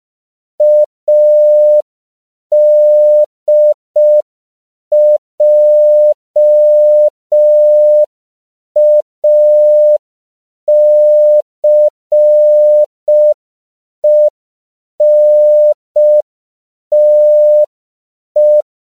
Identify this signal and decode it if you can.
Morse code, heard as a keyed tone, about 5 wpm, 600 Hz. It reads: ADJACENTE
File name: CW / 5PPM